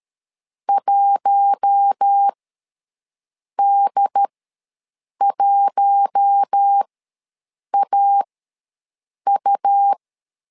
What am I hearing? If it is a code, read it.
Morse code, heard as a keyed tone, about 13 wpm, spaced as for 6 wpm, 800 Hz. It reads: 1D1AU
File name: LEZIONI / 4 Q1 ws 12wpm 2129